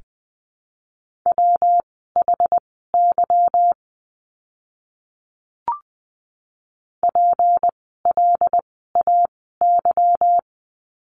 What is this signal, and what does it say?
Morse code, heard as a keyed tone, about 20 wpm, spaced as for 15 wpm, 700 Hz. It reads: WHY PLAY
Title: 244 Lesson 39f - Letter z - Words 20wpm